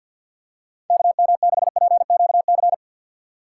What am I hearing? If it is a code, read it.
Morse code, heard as a keyed tone, about 48 wpm, 700 Hz. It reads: KM6PQC
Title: USA Call Signs - 6 Characters - 4x Character Spacing - 50wpm